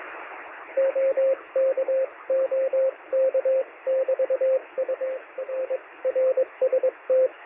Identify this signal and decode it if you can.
Morse code, heard as a keyed tone, about 22 wpm, 520 Hz, with radio noise. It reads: OK OK <BT> UR RST
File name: sm7lcb / 2114z oh3lwp 13cm